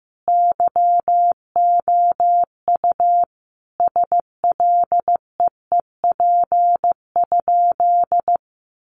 Morse code, heard as a keyed tone, about 15 wpm, 700 Hz. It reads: YOU SLEEP?
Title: Sentences from Top 500 Words 15wpm